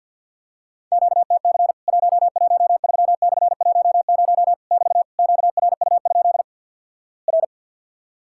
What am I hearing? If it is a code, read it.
Morse code, heard as a keyed tone, about 50 wpm, 700 Hz. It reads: QTC 113/10 <BT> QRU? R
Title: Worked All Europe DX Contest - QTC Exchange - 50wpm